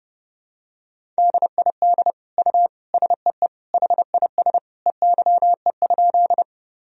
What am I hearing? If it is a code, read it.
Morse code, heard as a keyed tone, about 30 wpm, 700 Hz. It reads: DID U SEE HIS EYE?
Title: Sentences from Top 300 Words and Abbreviations 30wpm